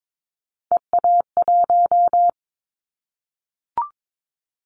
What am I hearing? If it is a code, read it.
Morse code, heard as a keyed tone, about 22 wpm, 700 Hz. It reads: EA1
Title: Callsign Prefixes for Top 25 DXCC - 22wpm